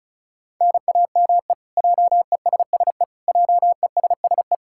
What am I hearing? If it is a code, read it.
Morse code, heard as a keyed tone, about 36 wpm, 700 Hz. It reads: NAME JESSE JESSE
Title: K1USN SST QSO Practice 35wpm